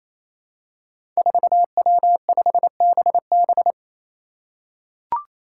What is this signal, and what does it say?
Morse code, heard as a keyed tone, about 28 wpm, 700 Hz. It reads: 4W5BB
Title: Five Challenging Letter-Number Combinations 28wpm